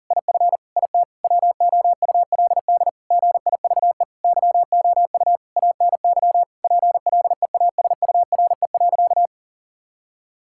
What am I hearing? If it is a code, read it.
Morse code, heard as a keyed tone, about 37 wpm, 700 Hz. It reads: IF IT WOULD GIVE YOU ANY PLEASURE.
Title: Sentences from Top 1600 Words 40wpm